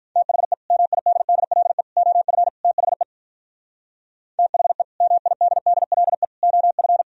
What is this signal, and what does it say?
Morse code, heard as a keyed tone, about 46 wpm, 700 Hz. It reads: THE MIDDLE OF THE THE MIDDLE OF